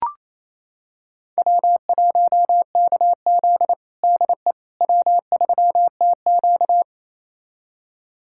Morse code, heard as a keyed tone, about 28 wpm, 700 Hz. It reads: W1KZ DE W3TQ
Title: USA Callsign DE Callsign - 4 Characters - 28wpm